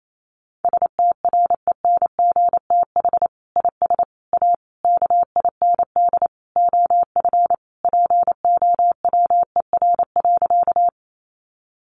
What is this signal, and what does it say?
Morse code, heard as a keyed tone, about 28 wpm, 700 Hz. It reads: STRENGTH IS A KIND OF POWER.
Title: Sentences from Top 1300 Words 28wpm